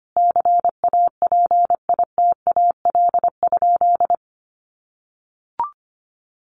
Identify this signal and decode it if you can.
Morse code, heard as a keyed tone, about 25 wpm, 700 Hz. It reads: CAPITAL?
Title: Sentences from Top 1100 Words 25wpm